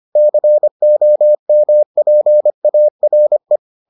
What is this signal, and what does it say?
Morse code, heard as a keyed tone, about 25 wpm, 600 Hz. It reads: COMPARE